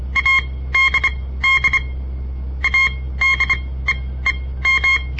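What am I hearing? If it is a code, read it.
Morse code, heard as a keyed tone, about 25 wpm, spaced as for 16 wpm, 2150 Hz. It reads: ADD ADEEM
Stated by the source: On-air Morse code practice broadcast by a southern Montana amateur radio association; all code, no read back.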